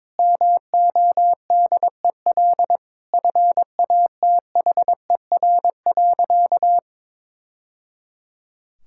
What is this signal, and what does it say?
Morse code, heard as a keyed tone, about 22 wpm, 700 Hz. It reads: MODEL FATHER.
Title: Sentences from Top 1000 Words - Review Part 5 - 22wpm